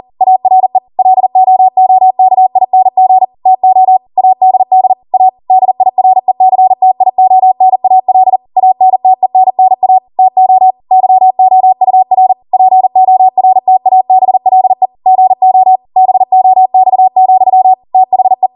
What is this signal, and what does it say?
Morse code, heard as a keyed tone, about 40 wpm, 750 Hz. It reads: ARE LOOKING TO ADD A DIRECTIONAL ANTENNA TO YOUR PORTABLE GO BOX, THE
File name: Morse / 40WPM